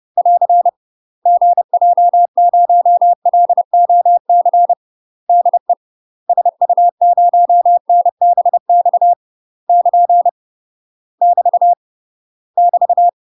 Morse code, heard as a keyed tone, about 30 wpm, 700 Hz. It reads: <AR> GJ0LOC DE SU0NBX <KN> <BT> <BT>
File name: UK MASTER QSOs TRAINING / 30WPM / UK MASTER QSOs 0040